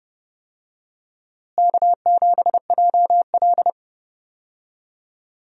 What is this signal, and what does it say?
Morse code, heard as a keyed tone, about 32 wpm, 700 Hz. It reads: K7JL
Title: USA Call Signs - 4 Characters - Repeated 2x - 30wpm